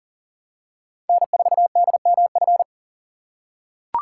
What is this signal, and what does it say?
Morse code, heard as a keyed tone, about 40 wpm, 700 Hz. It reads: N4DMF
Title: USA Call Signs - 5 Characters - 4x Character Spacing - 40wpm